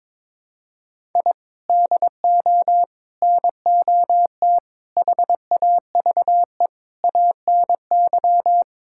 Morse code, heard as a keyed tone, about 22 wpm, 700 Hz. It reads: I DO NOT HAVE ANY
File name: practice / 22wpm / Build A Phrase - 5-Word Phrases - 22wpm